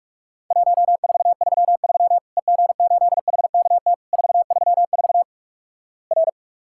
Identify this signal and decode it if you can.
Morse code, heard as a keyed tone, about 46 wpm, 700 Hz. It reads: 1433 EG8HKT 434 R
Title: Worked All Europe DX Contest - QTC Exchange - 45wpm